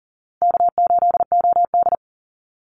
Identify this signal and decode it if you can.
Morse code, heard as a keyed tone, about 37 wpm, 700 Hz. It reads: K8OD
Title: Easy Call Signs Repeated 3x - 40wpm